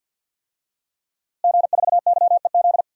Morse code, heard as a keyed tone, about 51 wpm, 700 Hz. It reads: G4YEZ
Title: DXCC-100 Call Signs - 5 Characters - Repeated 3x - 50wpm